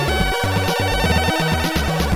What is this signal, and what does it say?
There is no code.